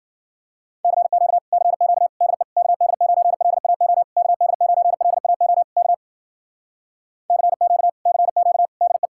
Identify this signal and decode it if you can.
Morse code, heard as a keyed tone, about 52 wpm, 700 Hz. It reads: CQ CQ DE KD9LAO KD9LAO K CQ CQ DE